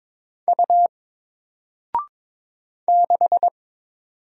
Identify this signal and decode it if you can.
Morse code, heard as a keyed tone, about 22 wpm, 700 Hz. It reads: U 6
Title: Single Letter-Number Alt 22wpm